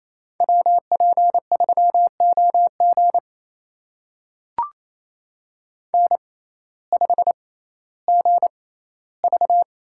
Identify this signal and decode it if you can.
Morse code, heard as a keyed tone, about 28 wpm, spaced as for 15 wpm, 700 Hz. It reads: WP3OG N5GV
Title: USA Call Signs - 6x Character Spacing - 28wpm